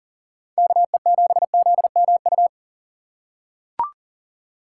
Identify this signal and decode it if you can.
Morse code, heard as a keyed tone, about 40 wpm, 700 Hz. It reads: KE7ZMU
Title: USA Call Signs - 6 Characters - 3x Character Spacing - 40wpm